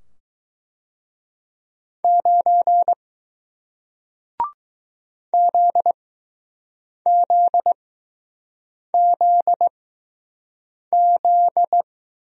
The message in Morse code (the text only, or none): 9 ZZZZ